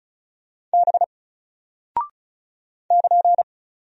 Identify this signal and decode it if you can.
Morse code, heard as a keyed tone, about 36 wpm, 700 Hz. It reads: B <KN>
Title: Single Character (Letter:Number:Punctuation:Prosign) Alt 35wpm